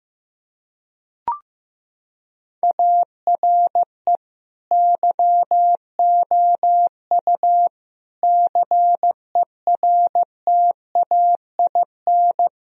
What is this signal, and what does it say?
Morse code, heard as a keyed tone, about 15 wpm, 700 Hz. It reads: ARE YOU CERTAIN
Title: Sentences from Top 1000 Words - Review Part 2 - 15wpm